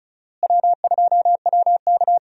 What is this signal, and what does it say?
Morse code, heard as a keyed tone, about 36 wpm, 700 Hz. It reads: W2WK
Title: USA Call Signs - 4 Characters - Repeated 3x - 35wpm